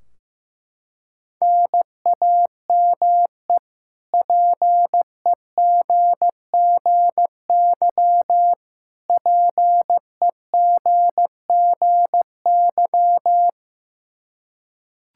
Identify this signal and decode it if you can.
Morse code, heard as a keyed tone, about 15 wpm, 700 Hz. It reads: NAME PEGGY PEGGY
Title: K1USN SST QSO Practice 15wpm